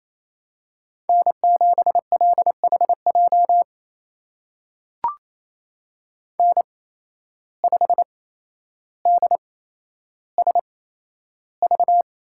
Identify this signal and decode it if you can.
Morse code, heard as a keyed tone, about 28 wpm, spaced as for 12 wpm, 700 Hz. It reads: N7LHJ N5DSV
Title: USA Call Signs - 5 Characters - 8x Character Spacing - 28wpm